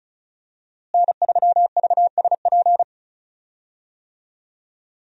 Morse code, heard as a keyed tone, about 34 wpm, 700 Hz. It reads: N3VSP
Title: USA Call Signs - 5 Characters - Repeated 2x - 35wpm